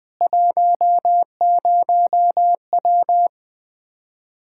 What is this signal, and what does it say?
Morse code, heard as a keyed tone, about 20 wpm, 700 Hz. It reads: 10W
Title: Practice QSO - Exchange 2 - Info - 20wpm